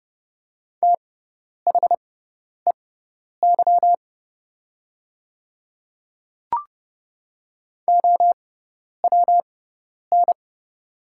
Morse code, heard as a keyed tone, about 30 wpm, spaced as for 10 wpm, 700 Hz. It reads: THEY OWN
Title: 6x Character Spacing - Top 500 Words - 30wpm